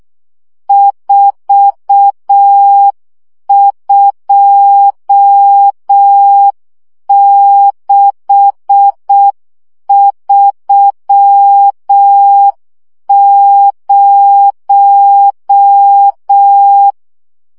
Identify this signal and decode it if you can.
Morse code, heard as a keyed tone, about 6 wpm, 800 Hz. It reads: 42630
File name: morze vizsga 2011-3 5